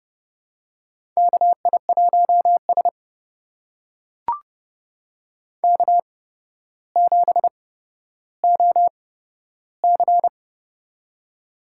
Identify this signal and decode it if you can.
Morse code, heard as a keyed tone, about 32 wpm, spaced as for 13 wpm, 700 Hz. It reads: KI1S K7OC
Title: USA Call Signs - 4 Characters - 8x Character Spacing - 30wpm